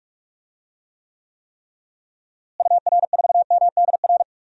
Morse code, heard as a keyed tone, about 43 wpm, 700 Hz. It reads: UR4MDR
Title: DXCC-100 Call Signs - 6 Characters - Repeated 3x - 45wpm